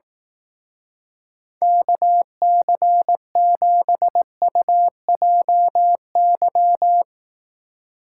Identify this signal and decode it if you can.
Morse code, heard as a keyed tone, about 18 wpm, 700 Hz. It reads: KC7UJY
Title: Speed Racing - USA 2x3 Call Signs 15wpm